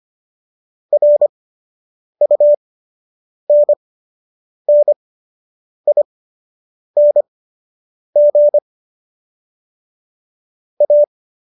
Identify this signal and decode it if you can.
Morse code, heard as a keyed tone, about 25 wpm, spaced as for 8 wpm, 590 Hz. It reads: RUNNING A